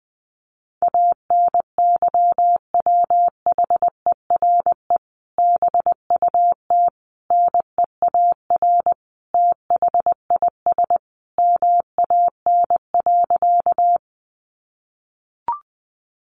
Morse code, heard as a keyed tone, about 20 wpm, 700 Hz. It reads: ANYWHERE BUT NEAR THIS MAN.
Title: Sentences from Top 1400 Words 20wpm